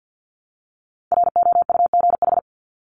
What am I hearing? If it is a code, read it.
Morse code, heard as a keyed tone, about 50 wpm, 700 Hz. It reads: ROUGH
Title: Top 1501-1750 Words 50wpm